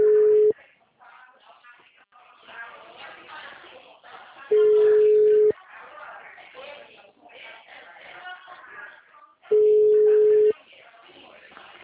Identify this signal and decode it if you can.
Ringback tone (Europe, 425 Hz), with speech.